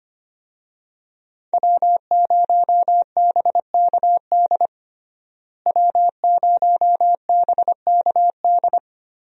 Morse code, heard as a keyed tone, about 25 wpm, 700 Hz. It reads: W0BKD W0BKD
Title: USA Call Signs - 5 Characters - Repeated 2x - 25wpm